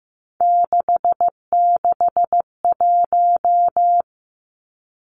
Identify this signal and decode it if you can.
Morse code, heard as a keyed tone, about 15 wpm, 700 Hz. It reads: 661
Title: Numbers - 3-Digit 15wpm